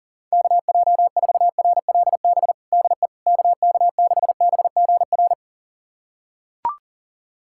Morse code, heard as a keyed tone, about 37 wpm, 700 Hz. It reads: KJ4RLB DE KK6BGR
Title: USA Callsign DE Callsign - 6 Characters - 40wpm